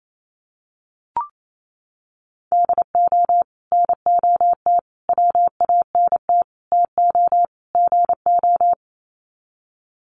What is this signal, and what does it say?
Morse code, heard as a keyed tone, about 28 wpm, 700 Hz. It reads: DO NOT WANT TO GO
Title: Sets of 5 Words 28wpm